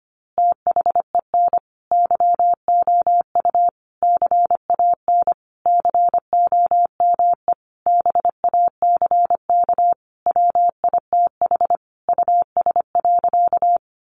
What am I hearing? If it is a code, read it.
Morse code, heard as a keyed tone, about 25 wpm, 700 Hz. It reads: THEN YOU CAN COME BACK WITH US.
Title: A Sound Of Thunder 25wpm